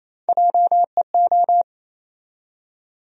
Morse code, heard as a keyed tone, about 28 wpm, 700 Hz. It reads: JEO
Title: USA Call Sign Postfixes - 3 Letters 28wpm